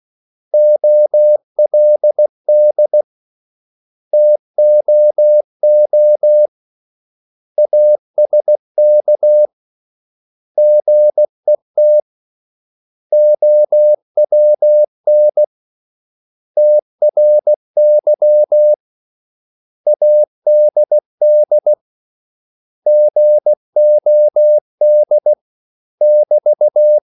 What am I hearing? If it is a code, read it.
Morse code, heard as a keyed tone, about 16 wpm, 600 Hz. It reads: OLD TOO ASK GET OWN TRY ADD GOD <BT>
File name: CW / Practice / 500MostCommonEnglishWords16